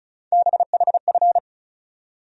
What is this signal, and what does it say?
Morse code, heard as a keyed tone, about 36 wpm, 700 Hz. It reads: BHF
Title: USA Call Sign Postfixes - 3 Letters 35wpm